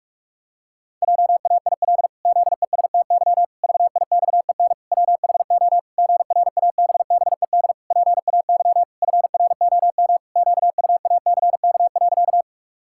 Morse code, heard as a keyed tone, about 46 wpm, 700 Hz. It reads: JAIL ZESTY VIXEN WHO GRABBED PAY FROM QUACK.
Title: Pangram Challenge v2 - 45wpm